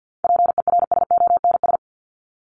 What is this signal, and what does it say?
Morse code, heard as a keyed tone, about 50 wpm, 700 Hz. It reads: PERSONS
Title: ES:S Suffix - Top 100 Words with Base Words 50wpm